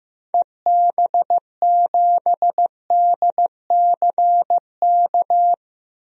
Morse code, heard as a keyed tone, about 15 wpm, 700 Hz. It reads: EB7DCK
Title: DXCC-100 Call Signs - 6 Characters - 15wpm